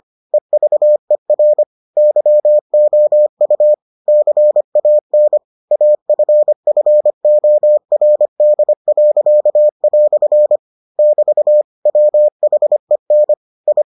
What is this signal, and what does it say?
Morse code, heard as a keyed tone, about 25 wpm, 600 Hz. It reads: EVER YOU CAN AFFORD." <BT> WHEN I